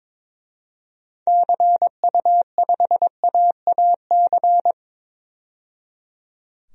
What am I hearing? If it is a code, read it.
Morse code, heard as a keyed tone, about 22 wpm, 700 Hz. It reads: CU5AAC
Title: DXCC-100 Call Signs - 6 Characters - 22wpm